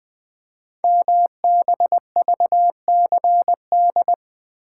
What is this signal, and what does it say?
Morse code, heard as a keyed tone, about 20 wpm, 700 Hz. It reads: MBVCD
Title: Code Groups - 5 Letters - 20wpm